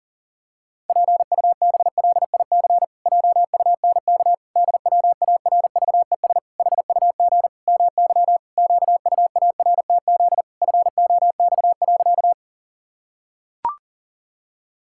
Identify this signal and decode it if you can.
Morse code, heard as a keyed tone, about 40 wpm, 700 Hz. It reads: PUBLIC JUNK DWARVES HUG MY QUARTZ FOX.
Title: Pangram Challenge v2 - 40wpm